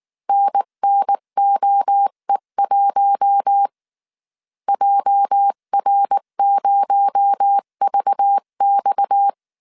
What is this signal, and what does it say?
Morse code, heard as a keyed tone, about 19 wpm, 800 Hz. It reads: NNOE1 JR0VX